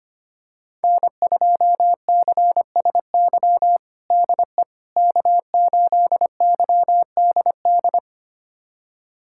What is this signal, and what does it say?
Morse code, heard as a keyed tone, about 25 wpm, 700 Hz. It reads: N2CSY DE K8YDD
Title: USA Callsign DE Callsign - 5 Characters - 25wpm